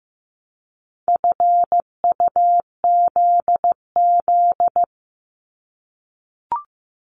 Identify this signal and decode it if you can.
Morse code, heard as a keyed tone, about 15 wpm, 690 Hz. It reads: FUZZ
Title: Multi-letter Phonemes 15wpm